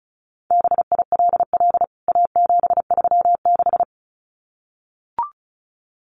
Morse code, heard as a keyed tone, about 34 wpm, 700 Hz. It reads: BILL A736
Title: CWT Exchange 35wpm